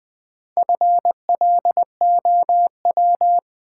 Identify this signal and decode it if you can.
Morse code, heard as a keyed tone, about 20 wpm, 700 Hz. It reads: FLOW
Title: Sentences from Top 1200 Words - New Word Drill Alt - 20wpm